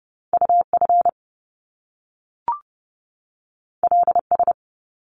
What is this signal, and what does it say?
Morse code, heard as a keyed tone, about 30 wpm, 700 Hz. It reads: UF LS
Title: Top Two Letter Combinations in Words 30wpm